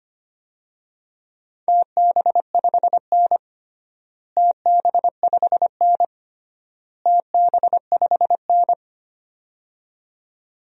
Morse code, heard as a keyed tone, about 25 wpm, 700 Hz. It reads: TB5N TB5N TB5N
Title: DXCC-100 Call Signs - 4 Characters - Repeated 3x - 25wpm